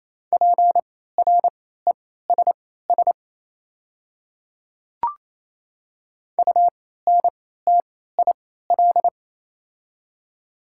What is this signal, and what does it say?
Morse code, heard as a keyed tone, about 27 wpm, spaced as for 16 wpm, 700 Hz. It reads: PRESS UNTIL
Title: 3x Character Spacing - Top 300 Words - 28wpm